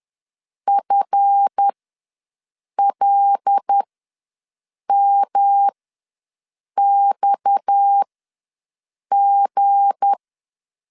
Morse code, heard as a keyed tone, about 10 wpm, spaced as for 6 wpm, 800 Hz. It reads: FLMXG
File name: LEZIONI / 7 GF9Z / 7 GF ws 10wpm 3918